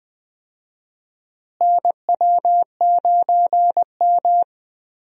Morse code, heard as a keyed tone, about 20 wpm, 700 Hz. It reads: NW9M
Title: USA Call Signs - 4 Characters - Repeated 3x - 20wpm